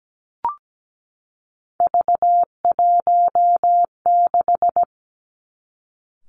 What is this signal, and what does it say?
Morse code, heard as a keyed tone, about 17 wpm, 700 Hz. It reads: V16
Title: Three Challenging Letter-Number Combinations 17wpm